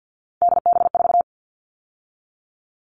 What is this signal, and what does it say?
Morse code, heard as a keyed tone, about 50 wpm, 700 Hz. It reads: DB4
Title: All Cut Numbers - 3-Digit 50wpm